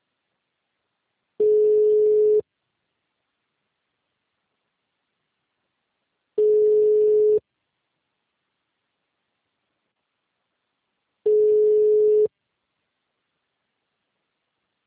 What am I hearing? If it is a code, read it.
Ringback tone (Europe, 425 Hz).
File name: out- -2004- - - .220